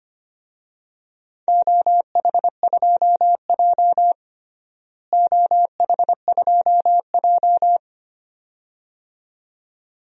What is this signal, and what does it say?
Morse code, heard as a keyed tone, about 25 wpm, 700 Hz. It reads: OH2J OH2J
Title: DXCC-100 Call Signs - 4 Characters - Repeated 2x - 25wpm